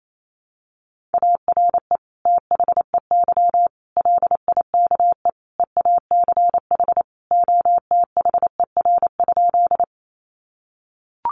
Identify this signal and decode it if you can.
Morse code, heard as a keyed tone, about 28 wpm, 700 Hz. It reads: ARE THEY LIKE EACH OTHER?
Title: Sentences from Top 200 Words 28wpm